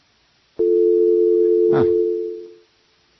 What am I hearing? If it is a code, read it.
Dial tone (North America), 350 + 440 Hz.